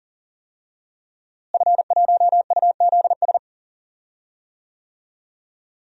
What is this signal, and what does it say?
Morse code, heard as a keyed tone, about 37 wpm, 700 Hz. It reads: F1UZS
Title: DXCC-100 Call Signs - 5 Characters - Repeated 3x - 40wpm